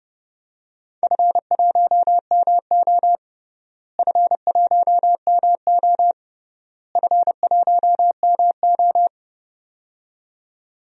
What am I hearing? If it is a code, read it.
Morse code, heard as a keyed tone, about 30 wpm, 700 Hz. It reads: F1MO F1MO F1MO
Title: DXCC-100 Call Signs - 4 Characters - Repeated 3x - 30wpm